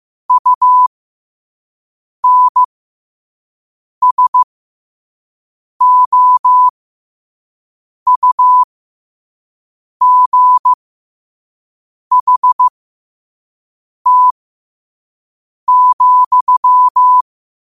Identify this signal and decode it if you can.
Morse code, heard as a keyed tone, about 15 wpm, spaced as for 5 wpm, 1000 Hz. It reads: UNSOUGHT,
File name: qotdfiles / 5wpm